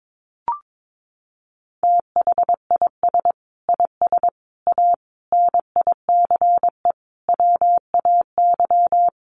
Morse code, heard as a keyed tone, about 22 wpm, 700 Hz. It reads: THIS IS A NICE WAY
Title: Sentences from Top 1100 Words 22wpm